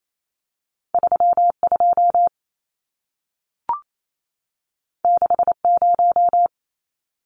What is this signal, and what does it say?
Morse code, heard as a keyed tone, about 28 wpm, 700 Hz. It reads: 32 60
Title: Numbers - 2-Digit 28wpm